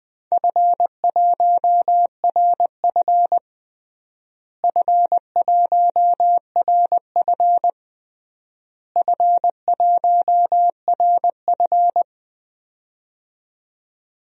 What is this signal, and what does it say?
Morse code, heard as a keyed tone, about 20 wpm, 700 Hz. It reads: F1RF F1RF F1RF
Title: DXCC-100 Call Signs - 4 Characters - Repeated 3x - 20wpm